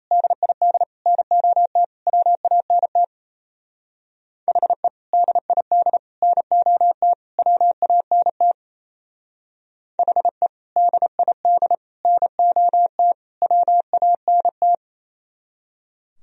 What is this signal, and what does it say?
Morse code, heard as a keyed tone, about 32 wpm, 700 Hz. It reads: DID NOT WANT HE DID NOT WANT HE DID NOT WANT
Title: Speed-Racing - Sets of 4 Words 28wpm